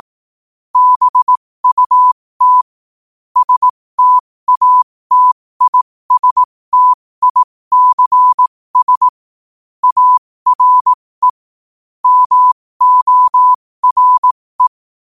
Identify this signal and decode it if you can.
Morse code, heard as a keyed tone, about 18 wpm, spaced as for 15 wpm, 1000 Hz. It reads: BUT STATISTICS ARE MORE
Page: New quotes every day in morse code at 15 Words per minute.